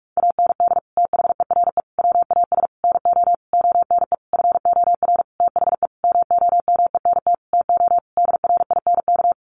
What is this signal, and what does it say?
Morse code, heard as a keyed tone, about 44 wpm, 700 Hz. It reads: AND THERE WAS NO ONE FOR THE MOMENT TO DRINK
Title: Sentences from Top 1500 Words 45wpm